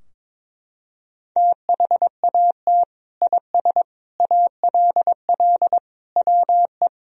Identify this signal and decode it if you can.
Morse code, heard as a keyed tone, about 22 wpm, 700 Hz. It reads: THAT IS ALL WE
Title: Sentences from Top 1000 Words - Review Part 2 - 22wpm